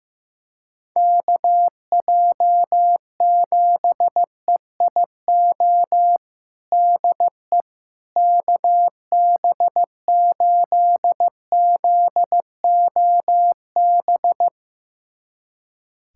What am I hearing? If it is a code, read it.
Morse code, heard as a keyed tone, about 15 wpm, 700 Hz. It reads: KJ7EIO DE KB8ZOB
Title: USA Callsign DE Callsign 15wpm